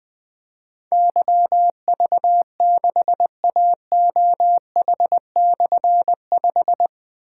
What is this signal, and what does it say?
Morse code, heard as a keyed tone, about 20 wpm, 700 Hz. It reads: YV6AOH/5
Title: Difficult Call Signs 20wpm